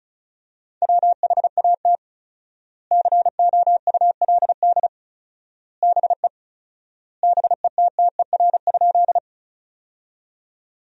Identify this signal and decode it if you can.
Morse code, heard as a keyed tone, about 36 wpm, 700 Hz. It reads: WHAT COULD BE BETTER?
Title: 4x Word-Spacing - Sentences from Top 400 Words - 35wpm